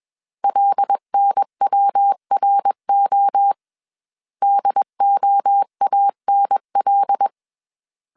Morse code, heard as a keyed tone, about 21 wpm, 800 Hz. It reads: LNWRO DOANL
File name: CW / DET / 1 6 1500C S B 20wpm 2048